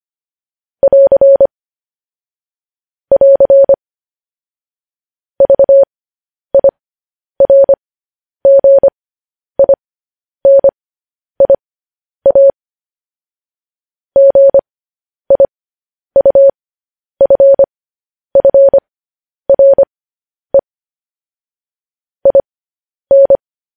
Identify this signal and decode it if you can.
Morse code, heard as a keyed tone, about 25 wpm, spaced as for 10 wpm, 550 Hz. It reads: <AR> <AR> VIRGINIA GIUFFRE IN
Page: news-headlines-in-morse-code-at-10-wpm